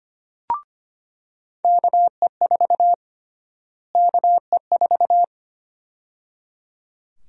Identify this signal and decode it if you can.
Morse code, heard as a keyed tone, about 25 wpm, 700 Hz. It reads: KE4 KE4
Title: USA Call Sign Prefixes - Repeated 2x - 25wpm